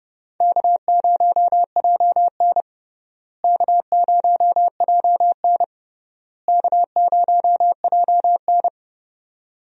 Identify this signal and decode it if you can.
Morse code, heard as a keyed tone, about 30 wpm, 700 Hz. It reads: K0JN K0JN K0JN
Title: USA Call Signs - 4 Characters - Repeated 3x - 30wpm